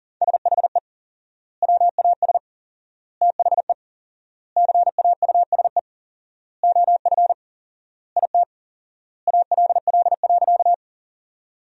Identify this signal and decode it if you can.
Morse code, heard as a keyed tone, about 40 wpm, 700 Hz. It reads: SHE WAS THE CAUSE OF IT ALL.
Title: 4x Word-Spacing - Sentences from Top 200 Words - 40wpm